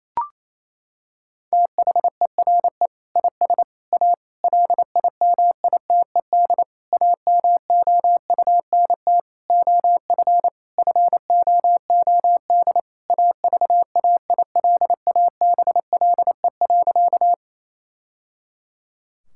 Morse code, heard as a keyed tone, about 27 wpm, 700 Hz. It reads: THERE IS A LIMITED AMOUNT OF FOOD AVAILABLE.
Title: Sentences from Top 1000 Words - Review Part 6 - 28wpm